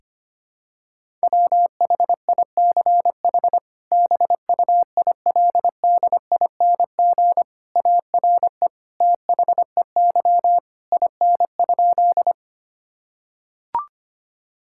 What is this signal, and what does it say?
Morse code, heard as a keyed tone, about 25 wpm, 700 Hz. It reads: WHICH BUILDING ARE THEY IN?
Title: Sentences from Top 1000 Words - Review Part 5 - 25wpm